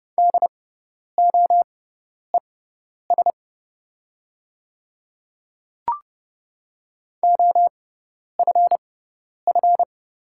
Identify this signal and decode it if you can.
Morse code, heard as a keyed tone, about 30 wpm, spaced as for 10 wpm, 700 Hz. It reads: DOES OFF
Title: 6x Character Spacing - Top 200 Words - 30wpm